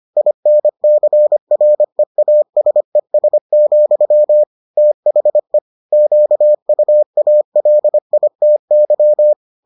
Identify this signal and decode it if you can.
Morse code, heard as a keyed tone, about 25 wpm, 600 Hz. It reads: INCREASES, THE QUALITY